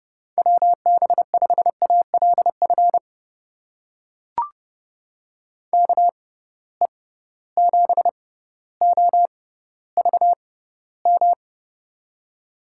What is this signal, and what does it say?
Morse code, heard as a keyed tone, about 30 wpm, spaced as for 15 wpm, 700 Hz. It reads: WB5ALF KE7OVM
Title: USA Call Signs - 6 Characters - 6x Character Spacing - 30wpm